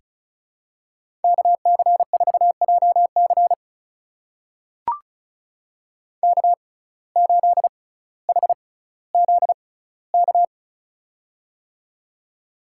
Morse code, heard as a keyed tone, about 36 wpm, spaced as for 18 wpm, 700 Hz. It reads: KC4JC K8HZK
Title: USA Call Signs - 6x Character Spacing - 35wpm